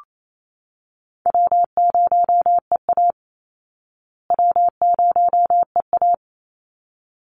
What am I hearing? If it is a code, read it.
Morse code, heard as a keyed tone, about 28 wpm, 700 Hz. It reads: W0EA W0EA
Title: Easy Call Signs Repeated 2x - 28wpm